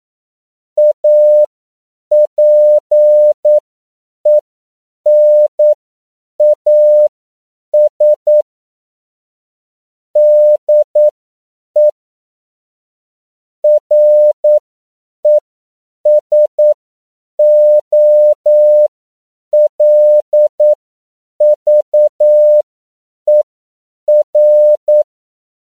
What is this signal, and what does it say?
Morse code, heard as a keyed tone, about 9 wpm, spaced as for 7 wpm, 600 Hz. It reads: APENAS DE RESOLVER